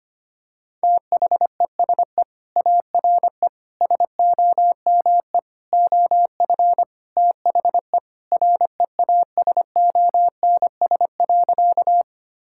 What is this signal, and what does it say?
Morse code, heard as a keyed tone, about 25 wpm, 700 Hz. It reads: THESE ARE SOME OF THE REASONS.
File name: practice / 25wpm / Sentences with 4-Word Phrases - 25wpm